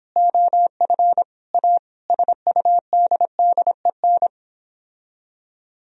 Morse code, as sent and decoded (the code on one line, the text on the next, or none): --- ..-. / .- / ... ..- -.. -.. -.
OF A SUDDEN